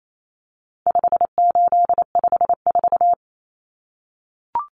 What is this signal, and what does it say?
Morse code, heard as a keyed tone, about 28 wpm, 700 Hz. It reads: 5854
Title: Common Cut Numbers - 4-Digit 28wpm